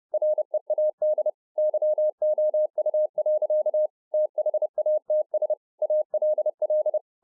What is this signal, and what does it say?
Morse code, heard as a keyed tone, about 30 wpm, 610 Hz. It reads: READ YOU. THATS ALL